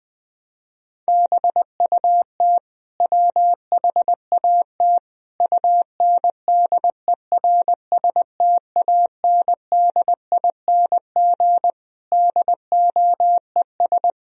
Morse code, heard as a keyed tone, about 20 wpm, 700 Hz. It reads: BUT WHAT UNDERSTANDING DOES
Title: Sentences from Top 1000 Words - Review Part 3 - 20wpm